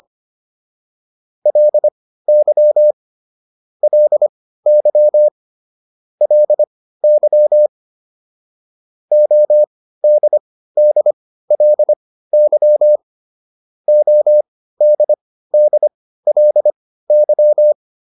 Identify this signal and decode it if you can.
Morse code, heard as a keyed tone, about 25 wpm, spaced as for 15 wpm, 600 Hz. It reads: LY LY LY ODDLY ODDLY